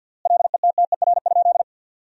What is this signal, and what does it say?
Morse code, heard as a keyed tone, about 52 wpm, 700 Hz. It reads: LETTER?
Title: 3x Word-Spacing - Sentences from Top 300 Words - 50wpm